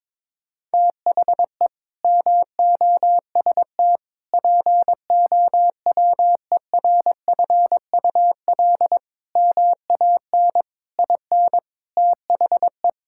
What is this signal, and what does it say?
Morse code, heard as a keyed tone, about 22 wpm, 700 Hz. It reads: THE MOST POWERFUL MAN IN THE
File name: practice / 22wpm / Sets of 7 Words - 22wpm